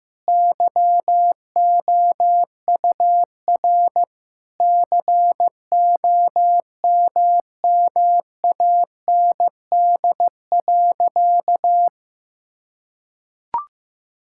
Morse code, heard as a keyed tone, about 15 wpm, 700 Hz. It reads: YOUR COMMAND.
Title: Sentences from Top 1600 Words 15wpm